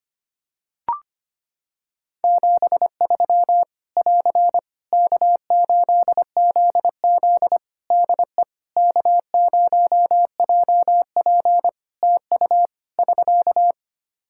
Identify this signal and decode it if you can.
Morse code, heard as a keyed tone, about 25 wpm, 700 Hz. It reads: 73 <AR> K8ZZ DE K0JP TU <SK>
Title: Practice QSO - Exchange 4 - Ending - 25wpm